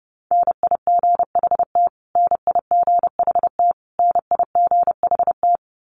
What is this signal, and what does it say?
Morse code, heard as a keyed tone, about 30 wpm, 700 Hz. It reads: NIGHT NIGHT NIGHT
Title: Top 300 Words Repeated 30wpm